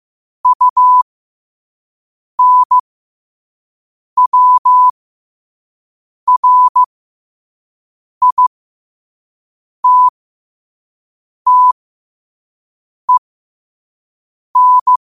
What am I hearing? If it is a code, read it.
Morse code, heard as a keyed tone, about 15 wpm, spaced as for 5 wpm, 1000 Hz. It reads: UNWRITTEN
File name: qotdfiles / 5wpm